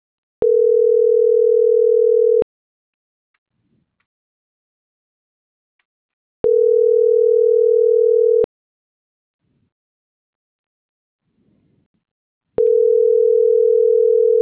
Ringback tone (North America), 440 + 480 Hz.